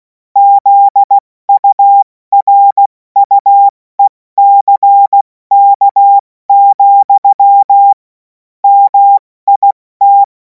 Morse code, heard as a keyed tone, about 16 wpm, 800 Hz. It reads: ZURUECK, MIT
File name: Morsekurs / Witze / Witz 07. April 2026 70 F80